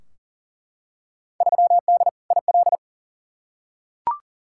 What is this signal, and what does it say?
Morse code, heard as a keyed tone, about 40 wpm, 700 Hz. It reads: 3D IL
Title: Field Day Exchanges Encore - 40wpm